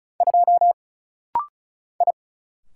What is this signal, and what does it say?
Morse code, heard as a keyed tone, about 34 wpm, 700 Hz. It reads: 2 I
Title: Single Letter-Number Alt - Rapid-Fire 35wpm